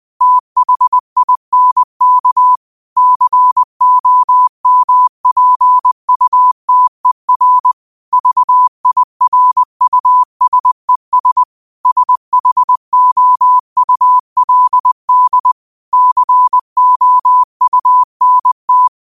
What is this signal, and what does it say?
Morse code, heard as a keyed tone, about 20 wpm, 1000 Hz. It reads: THINK COMPUTER VIRUSES SHOULD COUNT
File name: qotdfiles / 20wpm